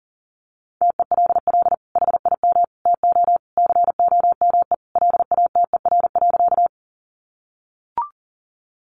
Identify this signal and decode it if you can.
Morse code, heard as a keyed tone, about 40 wpm, 700 Hz. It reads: TELL HIM TO COME LATER.
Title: Sentences from Top 700 Words 40wpm